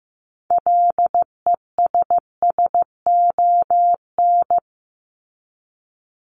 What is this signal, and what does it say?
Morse code, heard as a keyed tone, about 15 wpm, 700 Hz. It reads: LESSON
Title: Top 1001-1250 Words 15wpm